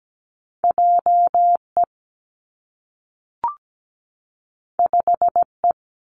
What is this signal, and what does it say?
Morse code, heard as a keyed tone, about 17 wpm, 700 Hz. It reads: JE 5E
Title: Two Challenging Letter-Number Combinations 17wpm